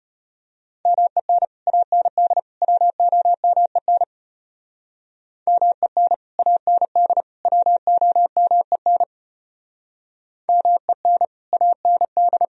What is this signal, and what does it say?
Morse code, heard as a keyed tone, about 31 wpm, 700 Hz. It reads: MEN AND WOMEN MEN AND WOMEN MEN AND
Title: Speed-Racing - Sets of 3 Words 25wpm